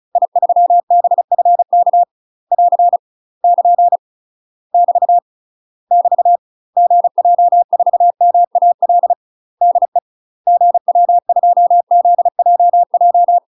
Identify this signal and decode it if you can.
Morse code, heard as a keyed tone, about 36 wpm, 700 Hz. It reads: I3BFK <AR> <KN> <BT> <BT> GJ4MAL DE GW2ZJJ
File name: UK MASTER QSOs TRAINING / 35WPM / UK MASTER QSOs 0023